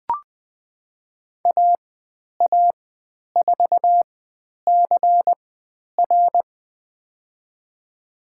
Morse code, heard as a keyed tone, about 20 wpm, spaced as for 10 wpm, 700 Hz. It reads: AA4CR
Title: 095 Lesson 16g - Letter c - Call Signs 20wpm